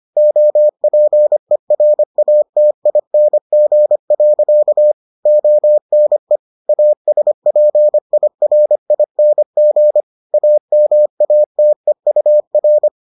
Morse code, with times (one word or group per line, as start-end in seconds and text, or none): 0.2-4.9 OPERATING.
5.3-6.4 ONE
6.7-10.0 ASPIRING
10.3-12.9 AMATEUR